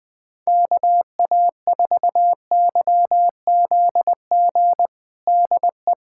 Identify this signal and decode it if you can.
Morse code, heard as a keyed tone, about 21 wpm, 700 Hz. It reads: KA4YZG DE
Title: USA Callsign DE Callsign - 6 Characters - 20wpm